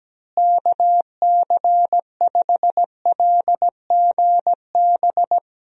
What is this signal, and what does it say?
Morse code, heard as a keyed tone, about 17 wpm, 700 Hz. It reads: KC5LGB
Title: USA Call Signs - 6 Characters - Repeated 3x - 17wpm